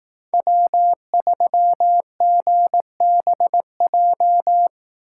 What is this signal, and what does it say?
Morse code, heard as a keyed tone, about 18 wpm, 700 Hz. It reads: W3GBJ
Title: Speed Racing - USA 1x3 Call Signs 15wpm